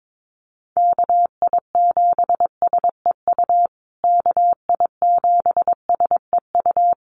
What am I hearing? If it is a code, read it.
Morse code, heard as a keyed tone, about 22 wpm, 700 Hz. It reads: KI7SEU KI7SEU
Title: USA Call Signs Repeated 3x 22wpm